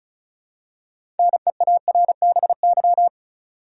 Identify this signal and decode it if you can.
Morse code, heard as a keyed tone, about 36 wpm, 700 Hz. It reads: NEARBY
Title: Six Letter Words - 35wpm